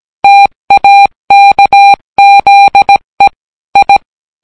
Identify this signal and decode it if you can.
Morse code, heard as a keyed tone, about 17 wpm, 800 Hz. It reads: TAKZE I